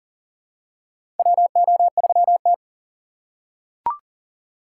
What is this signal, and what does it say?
Morse code, heard as a keyed tone, about 40 wpm, 700 Hz. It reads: WO3T
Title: USA Call Signs - 4 Characters - 4x Character Spacing - 40wpm